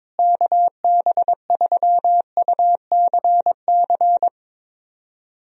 Morse code, heard as a keyed tone, about 22 wpm, 700 Hz. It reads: KB3UCC
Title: USA Call Signs - 6 Characters - Repeated 3x - 22wpm